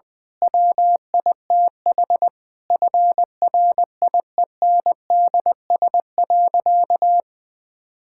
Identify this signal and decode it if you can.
Morse code, heard as a keyed tone, about 20 wpm, 700 Hz. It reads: WITH FRIENDS.
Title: Difficult Sentences with Hints - 20wpm